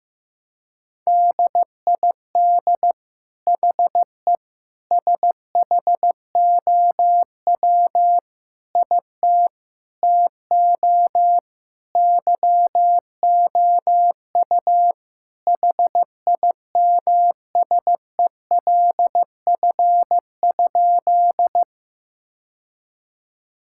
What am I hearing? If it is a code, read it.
Morse code, heard as a keyed tone, about 15 wpm, 700 Hz. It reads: DID HE SHOW IT TO YOU HIMSELF?
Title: Sentences from Top 1600 Words 15wpm